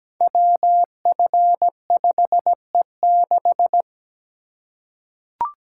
Morse code, heard as a keyed tone, about 17 wpm, 700 Hz. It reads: WF5E6
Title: Code Groups - 5 Letter:Numbers - 17wpm